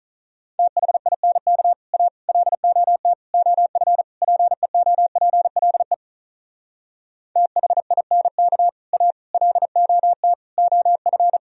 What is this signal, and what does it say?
Morse code, heard as a keyed tone, about 43 wpm, 700 Hz. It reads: THINK A LOT OF PEOPLE THINK A LOT OF
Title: Speed-Racing - Sets of 5 Words 35wpm